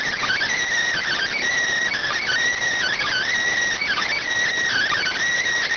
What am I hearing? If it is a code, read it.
Emergency Alert System (SAME) tones, heard as harsh buzzing bursts. It is heard from 1.0 to 5.2 s.